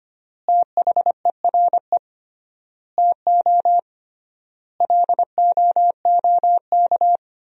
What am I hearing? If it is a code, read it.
Morse code, heard as a keyed tone, about 25 wpm, 700 Hz. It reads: THERE TO LOOK
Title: 3x Word-Spacing - Sentences from Top 100 Words - 25wpm